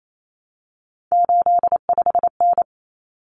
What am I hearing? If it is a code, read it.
Morse code, heard as a keyed tone, about 28 wpm, 700 Hz. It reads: 85N
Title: Common Cut Numbers - 3-Digit 28wpm